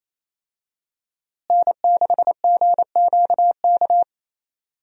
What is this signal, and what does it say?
Morse code, heard as a keyed tone, about 28 wpm, 700 Hz. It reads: N6GQK
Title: USA Call Signs - 5 Characters - Repeated 2x - 28wpm